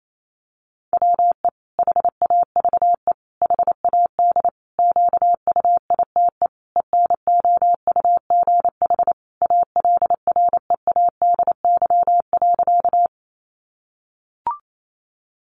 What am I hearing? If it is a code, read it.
Morse code, heard as a keyed tone, about 28 wpm, 700 Hz. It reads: WE HAVE HAD QUITE ENOUGH ALREADY.
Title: Sentences from Top 700 Words 28wpm